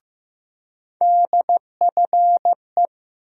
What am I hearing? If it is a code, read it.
Morse code, heard as a keyed tone, about 15 wpm, 700 Hz. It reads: DFE
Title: USA Call Sign Postfixes - 3 Letters 15wpm